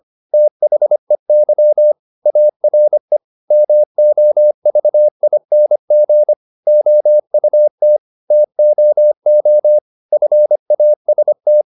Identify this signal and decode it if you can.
Morse code, heard as a keyed tone, about 25 wpm, 600 Hz. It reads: THEY ARE MOVING OUT TOO FAST
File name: the old man and the sea part 1 25wpm